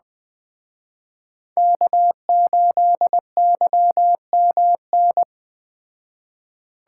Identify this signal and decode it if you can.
Morse code, heard as a keyed tone, about 20 wpm, 700 Hz. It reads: K8YMN